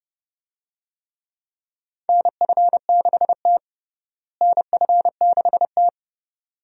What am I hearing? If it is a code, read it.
Morse code, heard as a keyed tone, about 29 wpm, 700 Hz. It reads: NF6T NF6T
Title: USA Call Signs - 4 Characters - Repeated 2x - 30wpm